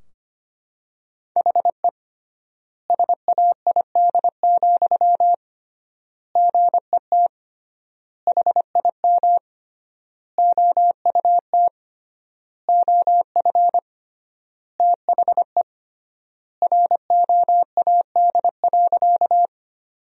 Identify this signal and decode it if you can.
Morse code, heard as a keyed tone, about 25 wpm, 700 Hz. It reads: HE SAID, GET HIM OUT OF THE ROAD.
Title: 3x Word-Spacing - Sentences from Top 400 Words - 25wpm